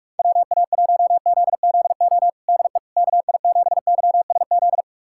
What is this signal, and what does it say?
Morse code, heard as a keyed tone, about 42 wpm, 700 Hz. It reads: WA1ZZO DE KI7YSZ
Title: USA Callsign DE Callsign - 6 Characters - 45wpm